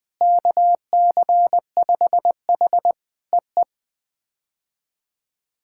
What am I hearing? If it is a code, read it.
Morse code, heard as a keyed tone, about 20 wpm, 700 Hz. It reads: KC5H EE
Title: POTA QSO Practice - 20wpm